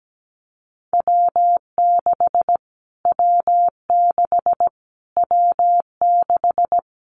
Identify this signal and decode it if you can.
Morse code, heard as a keyed tone, about 17 wpm, 700 Hz. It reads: W6 W6 W6
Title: Callsign Prefixes for Top 25 DXCC - Repeated 3x - 17wpm